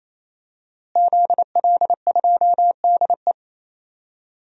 Morse code, heard as a keyed tone, about 28 wpm, 700 Hz. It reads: ZL2DE
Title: DXCC-100 Call Signs - 5 Characters - Repeated 3x - 28wpm